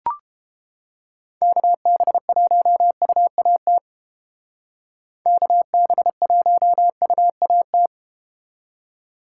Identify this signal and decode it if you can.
Morse code, heard as a keyed tone, about 31 wpm, 700 Hz. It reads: KB1UAT KB1UAT